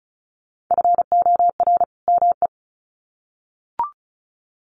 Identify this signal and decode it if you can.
Morse code, heard as a keyed tone, about 34 wpm, 700 Hz. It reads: FOR ME
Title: Sets of 2 Words 35wpm